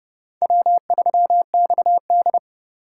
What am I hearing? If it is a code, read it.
Morse code, heard as a keyed tone, about 36 wpm, 700 Hz. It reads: W3XD
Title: Speed Racing - USA Call Signs 30wpm